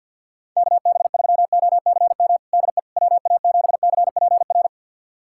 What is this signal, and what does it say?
Morse code, heard as a keyed tone, about 50 wpm, 690 Hz. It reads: KB3OCM DE WA7CPR
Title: USA Callsign DE Callsign 50wpm